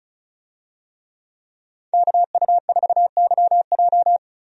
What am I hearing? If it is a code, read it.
Morse code, heard as a keyed tone, about 36 wpm, 700 Hz. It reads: KU4YJ
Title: USA Call Signs - 5 Characters - Repeated 2x - 35wpm